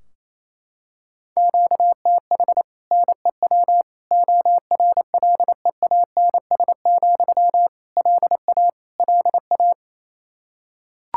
Morse code, heard as a keyed tone, about 28 wpm, 700 Hz. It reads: QTH NEW ORLEANS, LA LA
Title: Practice QSO - Exchange 1 - Beginning - 28wpm